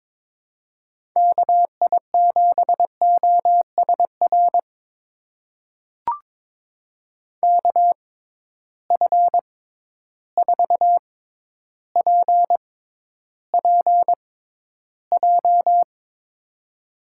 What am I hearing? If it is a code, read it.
Morse code, heard as a keyed tone, about 22 wpm, spaced as for 11 wpm, 700 Hz. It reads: KI7OSR KF4PPJ
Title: USA Call Signs - 6 Characters - 6x Character Spacing - 22wpm